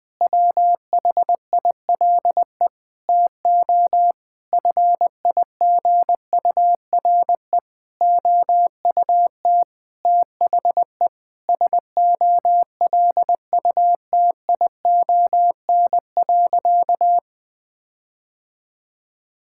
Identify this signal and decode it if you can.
Morse code, heard as a keyed tone, about 20 wpm, 700 Hz. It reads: WHILE TO FIGURE OUT THE SOLUTION.
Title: Sentences with 8-Word Phrases - 20wpm